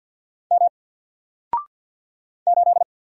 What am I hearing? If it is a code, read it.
Morse code, heard as a keyed tone, about 54 wpm, 700 Hz. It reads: M 8
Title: Single Character (Letter:Number:Punctuation:Prosign) 50wpm